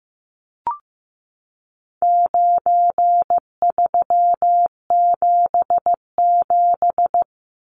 Morse code, heard as a keyed tone, about 15 wpm, 700 Hz. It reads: 9377
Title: Numbers - 4-Digit 15wpm